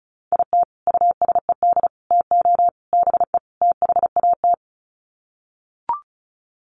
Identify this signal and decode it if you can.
Morse code, heard as a keyed tone, about 34 wpm, 700 Hz. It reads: IT USED TO BE THAT
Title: Sets of 5 Words Encore2 35wpm